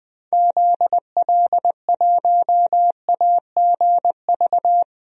Morse code, heard as a keyed tone, about 20 wpm, 700 Hz. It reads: ZL1AGV
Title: DXCC-100 Call Signs - 6 Characters - Repeated 3x - 20wpm